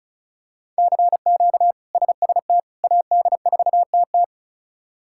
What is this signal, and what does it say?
Morse code, heard as a keyed tone, about 36 wpm, 700 Hz. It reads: CQ SST AD4TT
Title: K1USN SST QSO Practice - 35wpm